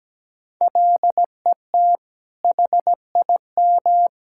Morse code, heard as a keyed tone, about 17 wpm, 700 Hz. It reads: LET HIM
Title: Sentences from Top 1000 Words - Review Part 6 - 17wpm